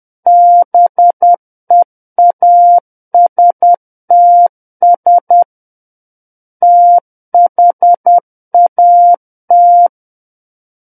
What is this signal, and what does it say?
Morse code, heard as a keyed tone, about 10 wpm, 700 Hz. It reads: BEASTS THAT
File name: Book One / 01-Chapter 1 - 10 WPM